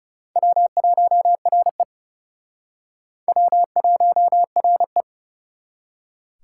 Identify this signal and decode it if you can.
Morse code, heard as a keyed tone, about 37 wpm, 700 Hz. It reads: W1RE W1RE
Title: Speed Racing - USA Call Signs 30wpm